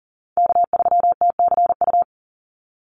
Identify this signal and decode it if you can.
Morse code, heard as a keyed tone, about 40 wpm, 700 Hz. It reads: K3TCU
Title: NA SOTA Chaser Callsigns 40wpm